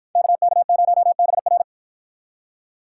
Morse code, heard as a keyed tone, about 53 wpm, 700 Hz. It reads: KK0BR